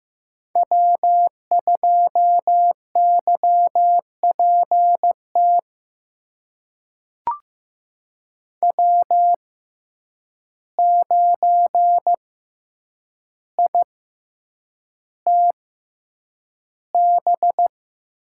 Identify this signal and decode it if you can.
Morse code, heard as a keyed tone, about 15 wpm, spaced as for 8 wpm, 700 Hz. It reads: W2YPT W9ITB
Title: USA Call Signs - 6x Character Spacing - 15wpm